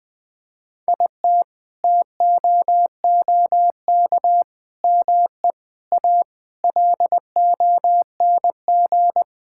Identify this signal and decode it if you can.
Morse code, heard as a keyed tone, about 21 wpm, 700 Hz. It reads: IT TOOK ME A LONG
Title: Sentences with 7-Word Phrases - 20wpm